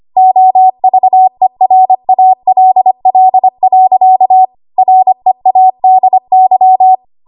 Morse code, heard as a keyed tone, about 25 wpm, 750 Hz. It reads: OVERALL. READY
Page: W1AW Web Code Practice Files